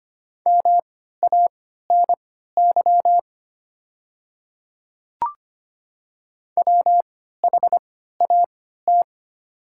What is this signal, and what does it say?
Morse code, heard as a keyed tone, about 25 wpm, spaced as for 14 wpm, 690 Hz. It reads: MANY WHAT
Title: 3x Character Spacing - Top 100 Words - 25wpm